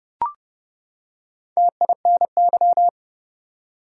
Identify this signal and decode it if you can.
Morse code, heard as a keyed tone, about 30 wpm, 700 Hz. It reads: TINY
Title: Four Letter Words - 30wpm